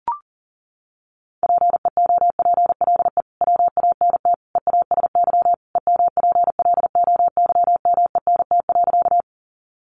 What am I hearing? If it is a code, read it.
Morse code, heard as a keyed tone, about 39 wpm, 700 Hz. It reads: PEOPLE WANT EASY EMPLOYMENT.
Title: Sentences from Top 1500 Words 40wpm